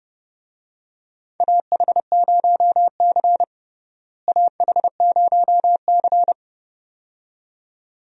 Morse code, heard as a keyed tone, about 30 wpm, 700 Hz. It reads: AH0C AH0C
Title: USA Call Signs - 4 Characters - Repeated 2x - 30wpm